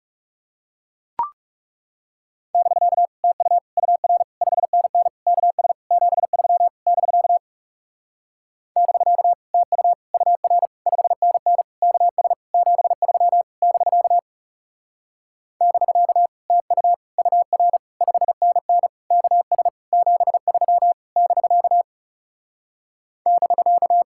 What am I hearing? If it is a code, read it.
Morse code, heard as a keyed tone, about 40 wpm, 700 Hz. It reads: <BK> TU UR 5NN KS 73 <BK> <BK> TU UR 5NN KS 73 <BK> <BK> TU UR 5NN KS 73 <BK> <BK>